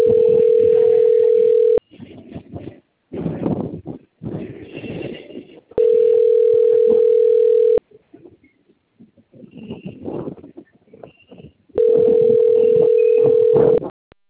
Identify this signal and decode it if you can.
Ringback tone (North America), 440 + 480 Hz.